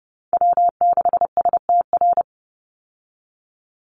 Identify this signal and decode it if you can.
Morse code, heard as a keyed tone, about 28 wpm, 700 Hz. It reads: W6STR
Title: Call Signs Repeated 3x - 30wpm